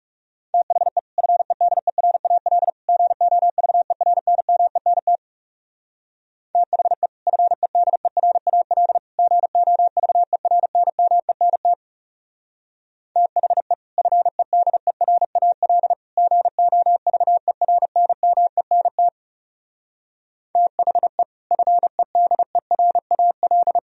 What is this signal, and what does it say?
Morse code, heard as a keyed tone, about 38 wpm, 700 Hz. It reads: THE FEDERAL GOVERNMENT THE FEDERAL GOVERNMENT THE FEDERAL GOVERNMENT THE FEDERAL